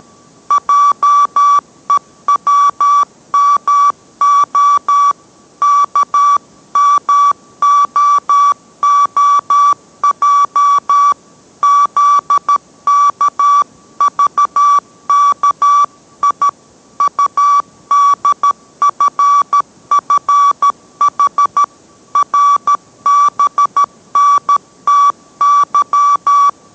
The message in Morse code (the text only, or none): JEWMO KMOOJ ZKVKI UDFFH RBNTY